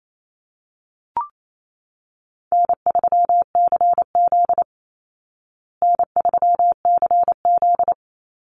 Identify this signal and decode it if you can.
Morse code, heard as a keyed tone, about 28 wpm, 700 Hz. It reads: N3CZ N3CZ
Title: Easy Call Signs Repeated 3x - 28wpm